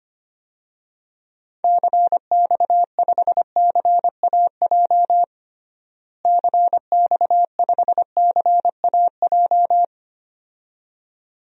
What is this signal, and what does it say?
Morse code, heard as a keyed tone, about 25 wpm, 700 Hz. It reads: CX5CAJ CX5CAJ
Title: DXCC-100 Call Signs - 6 Characters - Repeated 2x - 25wpm